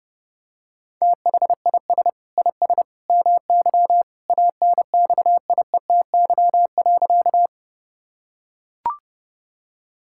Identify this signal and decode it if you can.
Morse code, heard as a keyed tone, about 30 wpm, 700 Hz. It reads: THIS IS MY ANXIETY.
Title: Sentences from Top 1600 Words 30wpm